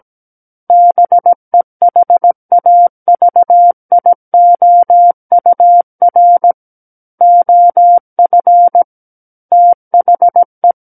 Morse code, heard as a keyed tone, about 17 wpm, 700 Hz. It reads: BEHAVIOUR OF THE
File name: Book One / 08-Chapter 8 - 17 WPM